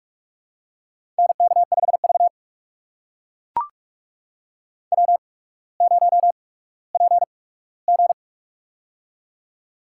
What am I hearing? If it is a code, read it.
Morse code, heard as a keyed tone, about 44 wpm, spaced as for 19 wpm, 700 Hz. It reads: NK5V W0PG
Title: USA Call Signs - 4 Characters - 8x Character Spacing - 45wpm